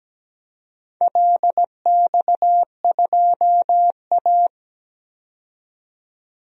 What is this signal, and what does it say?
Morse code, heard as a keyed tone, about 17 wpm, 700 Hz. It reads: LX2A
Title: DXCC-100 Call Signs - 4 Characters - 17wpm